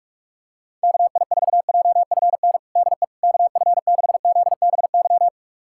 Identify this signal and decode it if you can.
Morse code, heard as a keyed tone, about 42 wpm, 700 Hz. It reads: KI4JFN DE KF6ZBY
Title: USA Callsign DE Callsign - 6 Characters - 45wpm